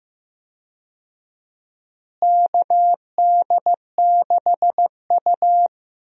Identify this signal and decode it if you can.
Morse code, heard as a keyed tone, about 15 wpm, 700 Hz. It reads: KD6U
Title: USA Call Signs - 4 Characters - Repeated 3x - 15wpm